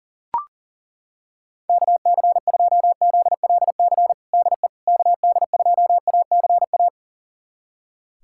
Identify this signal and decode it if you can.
Morse code, heard as a keyed tone, about 37 wpm, 700 Hz. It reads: KC2ZLC DE KD2ACA
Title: USA Callsign DE Callsign 40wpm